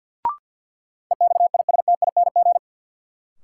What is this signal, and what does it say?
Morse code, heard as a keyed tone, about 50 wpm, 700 Hz. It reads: EXISTING
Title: Sentences from Top 1500 Words - New Word Drill - 50wpm